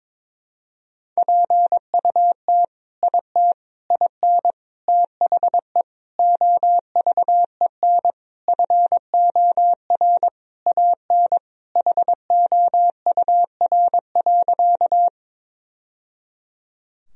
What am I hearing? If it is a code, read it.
Morse code, heard as a keyed tone, about 22 wpm, 700 Hz. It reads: PUT IT IN THE OVEN FOR AN HOUR.
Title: Sentences from Top 1000 Words - Review Part 4 - 22wpm